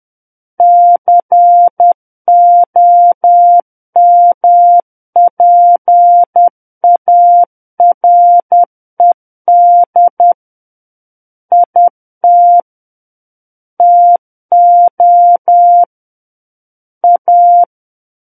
Morse code, heard as a keyed tone, about 10 wpm, 700 Hz. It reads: COMPARED IT TO A
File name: Book One / 01-Chapter 1 - 10 WPM